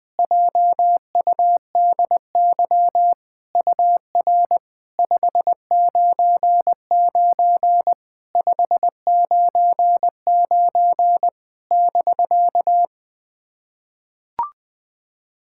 Morse code, heard as a keyed tone, about 20 wpm, 700 Hz. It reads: JUDY UR 599 599 <BK>
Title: SOTA QSO Practice 20wpm